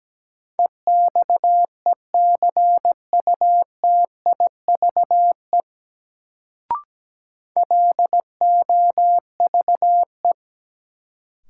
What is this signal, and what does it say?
Morse code, heard as a keyed tone, about 17 wpm, 700 Hz. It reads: EXECUTIVE LOVE
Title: Sets of 5 Words Encore - New Word Drill - 17wpm